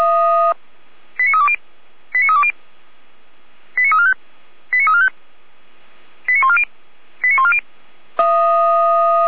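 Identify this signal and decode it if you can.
Five-tone selective call (ZVEI), heard as quick runs of five tones, tones of about 70 ms. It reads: 89320 89320 89345 89345 89140 89140
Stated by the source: ZVEI five-tone selective call received on SDR, fire brigade alerting, Feuerwehr Orlamünde.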